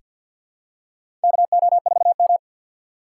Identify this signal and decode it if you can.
Morse code, heard as a keyed tone, about 48 wpm, 700 Hz. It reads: KO4M
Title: USA Call Signs - 4 Characters - 2x Character Spacing - 50wpm